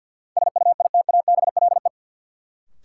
Morse code, heard as a keyed tone, about 53 wpm, 700 Hz. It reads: SUITABLE
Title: Sentences from Top 1400 Words - New Word Drill Alt - 50wpm